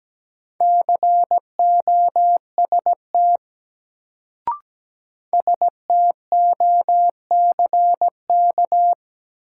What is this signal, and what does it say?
Morse code, heard as a keyed tone, about 17 wpm, 700 Hz. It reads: COST STOCK
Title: Sentences from Top 800 Words - New Word Drill - 17wpm